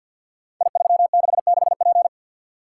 Morse code, heard as a keyed tone, about 47 wpm, 700 Hz. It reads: I366P
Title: Code Groups - 5 Letter:Numbers - 50wpm